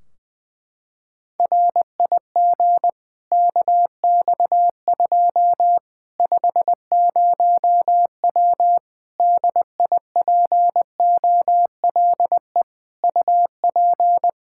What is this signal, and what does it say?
Morse code, heard as a keyed tone, about 20 wpm, 700 Hz. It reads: RIG KX2 50W DIPOLE UP
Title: K1USN SST QSO Practice 20wpm